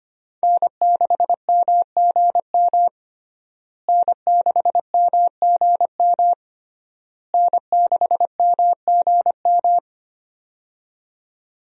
Morse code, heard as a keyed tone, about 25 wpm, 700 Hz. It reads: N6MGM N6MGM N6MGM
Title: USA Call Signs - 5 Characters - Repeated 3x - 25wpm